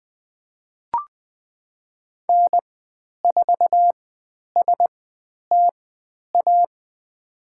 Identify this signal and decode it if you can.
Morse code, heard as a keyed tone, about 20 wpm, spaced as for 10 wpm, 700 Hz. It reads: N4STA
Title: 095 Lesson 16g - Letter c - Call Signs 20wpm